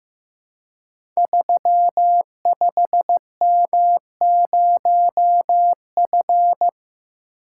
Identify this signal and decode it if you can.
Morse code, heard as a keyed tone, about 15 wpm, 700 Hz. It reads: 35M0F
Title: Code Groups - 5 Letter:Numbers - 15wpm